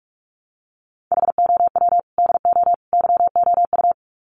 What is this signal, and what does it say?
Morse code, heard as a keyed tone, about 44 wpm, 700 Hz. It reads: HOW DO YOU
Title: Sets of 3 Words 45wpm